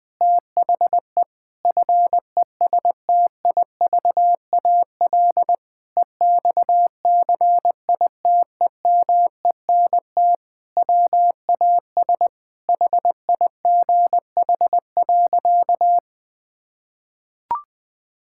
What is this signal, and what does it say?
Morse code, heard as a keyed tone, about 21 wpm, 700 Hz. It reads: THE FESTIVAL EXCITEMENT WAS HIGH.
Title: Sentences with 5-Word Phrases - 20wpm